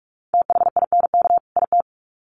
Morse code, heard as a keyed tone, about 44 wpm, 700 Hz. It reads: THINK IT
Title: Sets of 2 Words 45wpm